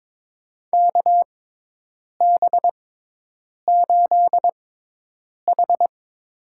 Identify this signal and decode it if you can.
Morse code, heard as a keyed tone, about 22 wpm, spaced as for 8 wpm, 700 Hz. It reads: KB8H
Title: USA Call Signs - 6 Characters - 6x Character Spacing - 22wpm